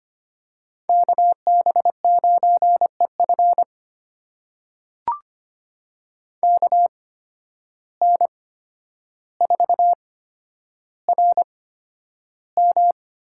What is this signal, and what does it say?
Morse code, heard as a keyed tone, about 25 wpm, spaced as for 11 wpm, 700 Hz. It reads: KB9EF KN4RM
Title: USA Call Signs - 5 Characters - 8x Character Spacing - 25wpm